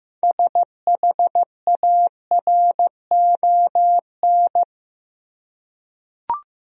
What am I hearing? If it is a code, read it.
Morse code, heard as a keyed tone, about 15 wpm, 700 Hz. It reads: SHARON
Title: USA Common Names - 15wpm